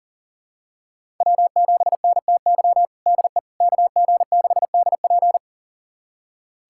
Morse code, heard as a keyed tone, about 40 wpm, 700 Hz. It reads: W7NTY DE KG6DP
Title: USA Callsign DE Callsign - 5 Characters - 40wpm